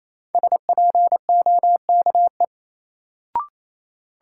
Morse code, heard as a keyed tone, about 28 wpm, 700 Hz. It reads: SPOKE
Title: Sentences from Top 1200 Words - New Word Drill - 28wpm